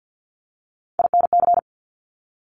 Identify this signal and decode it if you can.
Morse code, heard as a keyed tone, about 50 wpm, 700 Hz. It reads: INC
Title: Top Three Letter Combinations in Words 50wpm